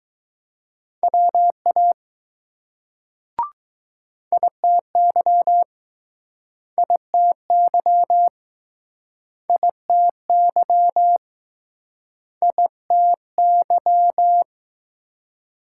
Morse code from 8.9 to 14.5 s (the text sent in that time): ITY ITY